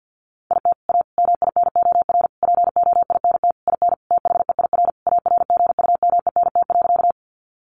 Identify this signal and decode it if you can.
Morse code, heard as a keyed tone, about 48 wpm, 700 Hz. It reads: IT A MINOR POINT IN THEIR ARGUMENT.
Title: Sentences from Top 1500 Words 50wpm